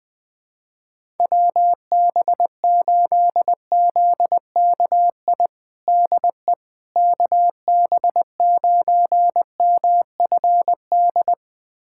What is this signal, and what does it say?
Morse code, heard as a keyed tone, about 20 wpm, 700 Hz. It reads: WB8ZKI DE KB9MFD
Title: USA Callsign DE Callsign - 6 Characters - 20wpm